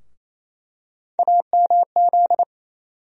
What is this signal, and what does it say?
Morse code, heard as a keyed tone, about 28 wpm, 700 Hz. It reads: AMZ
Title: USA Call Sign Postfixes - 3 Letters 28wpm